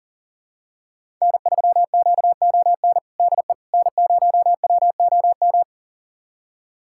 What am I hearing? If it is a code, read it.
Morse code, heard as a keyed tone, about 37 wpm, 700 Hz. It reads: N3QON DE N0WOM
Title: USA Callsign DE Callsign - 5 Characters - 40wpm